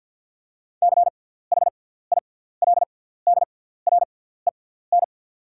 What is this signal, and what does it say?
Morse code, heard as a keyed tone, about 48 wpm, spaced as for 17 wpm, 690 Hz. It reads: CHILDREN
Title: 6x Character Spacing - Top 300 Words - 50wpm